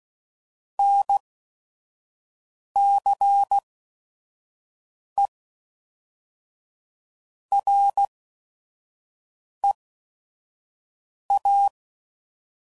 Morse code, heard as a keyed tone, about 16 wpm, spaced as for 5 wpm, 800 Hz. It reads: NCE REA
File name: CW Code Groups @5WPM / Technical Subjects / Reactance